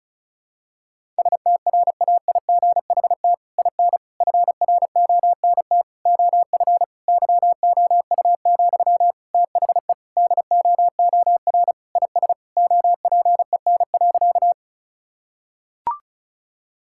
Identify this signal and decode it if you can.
Morse code, heard as a keyed tone, about 36 wpm, 700 Hz. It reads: STRAIGHT IN FRONT OF YOU, THE DOOR IS OPEN.
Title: Sentences from Top 1000 Words - Review Part 6 - 35wpm